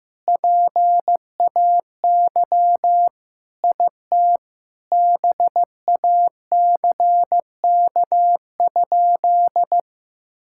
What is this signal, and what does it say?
Morse code, heard as a keyed tone, about 16 wpm, 700 Hz. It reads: PAY IT BACK?
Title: Sentences from Top 1600 Words 15wpm